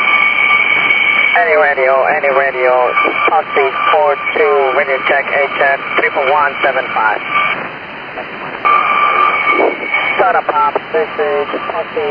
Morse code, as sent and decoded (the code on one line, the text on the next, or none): none